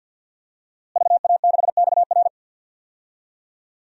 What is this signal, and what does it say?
Morse code, heard as a keyed tone, about 52 wpm, 700 Hz. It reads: VA6XR
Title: DXCC-100 Call Signs - 5 Characters - Repeated 3x - 50wpm